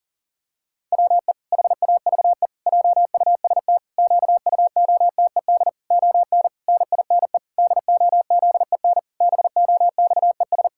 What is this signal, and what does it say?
Morse code, heard as a keyed tone, about 40 wpm, 700 Hz. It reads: WE HAVE JUST QUOTED ON NINE DOZEN BOXES
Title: Pangram Challenge v2 - 40wpm